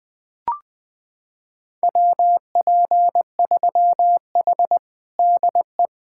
Morse code, heard as a keyed tone, about 20 wpm, 700 Hz. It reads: WP3H DE
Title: USA Callsign DE Callsign 20wpm